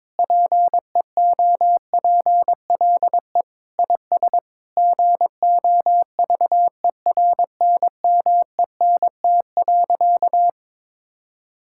Morse code, heard as a keyed tone, about 22 wpm, 700 Hz. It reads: PEOPLE IS GOVERNMENT.
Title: Sentences from Top 1000 Words - Review Part 6 - 22wpm